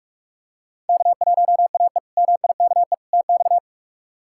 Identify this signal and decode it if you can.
Morse code, heard as a keyed tone, about 41 wpm, 700 Hz. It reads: K1AE MIKE TX
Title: K1USN SST QSO Practice - 45wpm